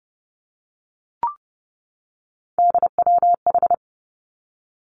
Morse code, heard as a keyed tone, about 30 wpm, 700 Hz. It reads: DWH
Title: Three Challenging Letter-Number Combinations 30wpm